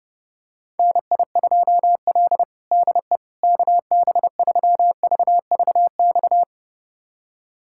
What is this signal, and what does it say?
Morse code, heard as a keyed tone, about 30 wpm, 700 Hz. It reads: NI2L DE KB3VVX
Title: USA Callsign DE Callsign 30wpm